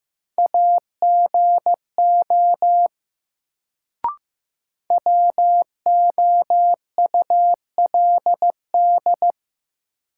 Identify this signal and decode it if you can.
Morse code, heard as a keyed tone, about 15 wpm, 700 Hz. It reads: AGO WOULD
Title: Sets of 2 Words - New Word Drill - 15wpm